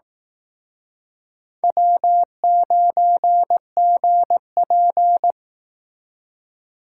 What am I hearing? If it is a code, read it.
Morse code, heard as a keyed tone, about 19 wpm, 700 Hz. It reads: W9GP
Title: Speed Racing - USA 1x2 Call Signs 15wpm